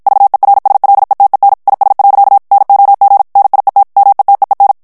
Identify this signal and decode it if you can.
Morse code, heard as a keyed tone, about 56 wpm, 800 Hz. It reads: VERSLETEN IS. NOG NIET METEEN
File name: morse mp3 / HOLLANDAIS texte clair 2 vitesse 50 mots minute